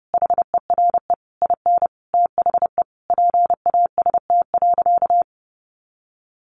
Morse code, heard as a keyed tone, about 30 wpm, 700 Hz. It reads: HERE IN THE PAST.
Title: A Sound Of Thunder 30wpm